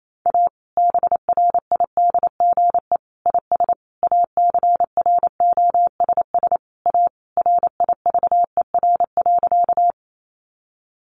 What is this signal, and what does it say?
Morse code, heard as a keyed tone, about 28 wpm, 700 Hz. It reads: A BRIDGE IS ACROSS A RIVER.
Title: Sentences from Top 1200 Words 28wpm